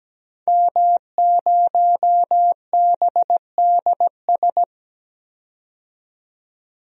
Morse code, heard as a keyed tone, about 17 wpm, 700 Hz. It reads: M0BDS
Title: DXCC-100 Call Signs - 5 Characters - 17wpm